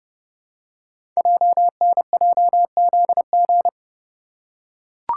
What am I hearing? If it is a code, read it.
Morse code, heard as a keyed tone, about 30 wpm, 700 Hz. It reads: JNJZG
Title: Code Groups - 5 Letters - 30wpm